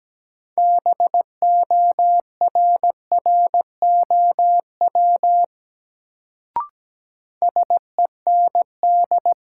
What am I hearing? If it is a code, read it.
Morse code, heard as a keyed tone, about 17 wpm, 700 Hz. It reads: BORROW SEND
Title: Sentences from Top 1500 Words - New Word Drill - 17wpm